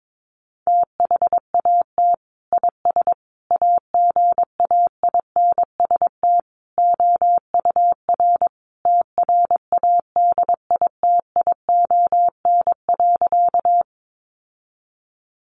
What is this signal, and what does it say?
Morse code, heard as a keyed tone, about 22 wpm, 700 Hz. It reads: THAT IS AGAINST OUR TRADITION.
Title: Sentences from Top 1500 Words 22wpm